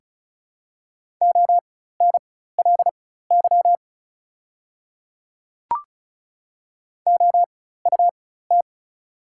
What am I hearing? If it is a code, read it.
Morse code, heard as a keyed tone, about 36 wpm, spaced as for 16 wpm, 700 Hz. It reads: ONLY OUT
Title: 4x Character Spacing - Top 200 Words - 35wpm